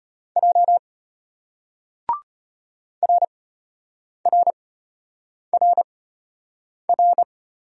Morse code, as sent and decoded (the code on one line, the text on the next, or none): .--- / .-. .-. .-. .-.
J RRRR